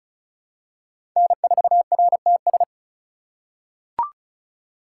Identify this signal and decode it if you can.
Morse code, heard as a keyed tone, about 36 wpm, 700 Hz. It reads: N4RTS
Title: USA Call Signs - 5 Characters - 8x Character Spacing - 35wpm